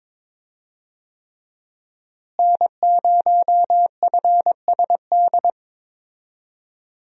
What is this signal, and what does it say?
Morse code, heard as a keyed tone, about 22 wpm, 700 Hz. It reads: N0FSD
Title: USA Call Signs - 5 Characters - Repeated 3x - 22wpm